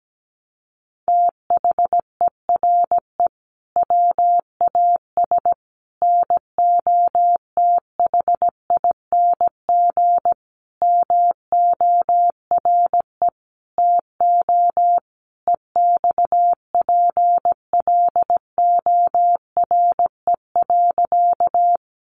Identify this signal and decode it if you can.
Morse code, heard as a keyed tone, about 17 wpm, 700 Hz. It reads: THERE WAS NOTHING MORE TO EXPLORE.
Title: Sentences from Top 1100 Words 17wpm